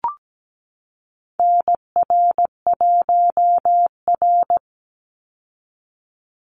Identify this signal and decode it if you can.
Morse code, heard as a keyed tone, about 17 wpm, 700 Hz. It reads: NR1R
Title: Call Signs Repeated 2x - 17wpm